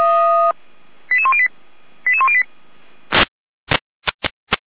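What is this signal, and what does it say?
Five-tone selective call (ZVEI), heard as quick runs of five tones, tones of about 70 ms. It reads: 88108 88108
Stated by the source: ZVEI five-tone selective call received by SDR.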